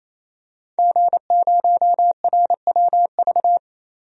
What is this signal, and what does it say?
Morse code, heard as a keyed tone, about 28 wpm, 700 Hz. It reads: G0RWV
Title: DXCC-100 Call Signs - 5 Characters - 28wpm